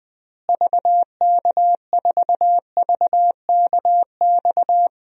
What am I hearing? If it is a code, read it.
Morse code, heard as a keyed tone, about 20 wpm, 700 Hz. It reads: VK4VKX
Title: DXCC-100 Call Signs - 6 Characters - Repeated 2x - 20wpm